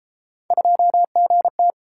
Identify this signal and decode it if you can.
Morse code, heard as a keyed tone, about 31 wpm, 700 Hz. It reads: 2GT